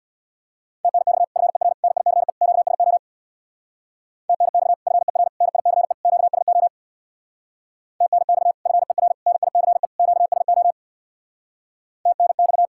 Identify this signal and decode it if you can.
Morse code, heard as a keyed tone, about 60 wpm, 700 Hz. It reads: TNX FER NICE QSO TNX FER NICE QSO TNX FER NICE QSO TNX
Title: Speed-Racing Common QSO Phrases - 50wpm